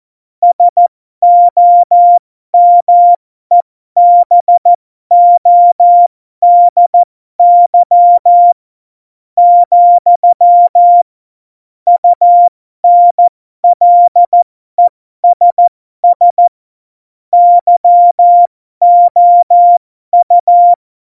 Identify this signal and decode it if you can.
Morse code, heard as a keyed tone, about 14 wpm, spaced as for 12 wpm, 700 Hz. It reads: SOMEBODY, UNLESS YOU
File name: Morsekurs / Witze